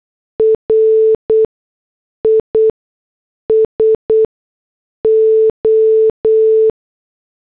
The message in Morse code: RISO